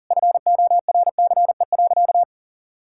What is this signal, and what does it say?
Morse code, heard as a keyed tone, about 40 wpm, 700 Hz. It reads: FORCE.
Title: Sentences from Top 1000 Words - Review Part 6 - 40wpm